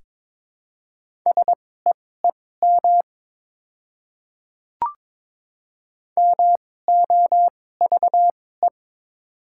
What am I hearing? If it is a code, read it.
Morse code, heard as a keyed tone, about 21 wpm, spaced as for 16 wpm, 700 Hz. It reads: SEEM MOVE
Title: 2x Character Spacing - Top 300 Words - 22wpm